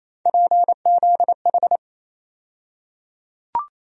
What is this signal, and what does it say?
Morse code, heard as a keyed tone, about 28 wpm, 700 Hz. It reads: PZH
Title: USA Call Sign Postfixes - 3 Letters 28wpm